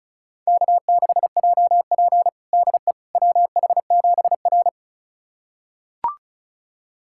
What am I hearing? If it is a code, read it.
Morse code, heard as a keyed tone, about 36 wpm, 700 Hz. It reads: K6JP DE WH7R
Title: USA Callsign DE Callsign - 4 Characters - 35wpm